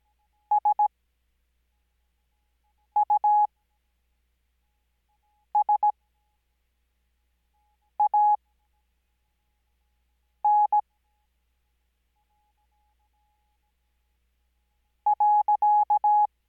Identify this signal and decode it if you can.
Morse code, heard as a keyed tone, about 17 wpm, spaced as for 4 wpm, 840 Hz. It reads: SUSAN .
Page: Farnsworth Spacing: 15WPM Character speed Spaced at 5WPM.